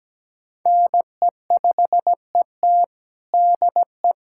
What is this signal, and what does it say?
Morse code, heard as a keyed tone, about 17 wpm, 700 Hz. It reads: NE5ET DE
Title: USA Callsign DE Callsign - 5 Characters - 17wpm